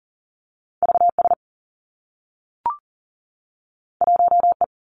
Morse code, heard as a keyed tone, about 39 wpm, 700 Hz. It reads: V S 1 E
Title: Two Challenging Letter-Number Combinations 40wpm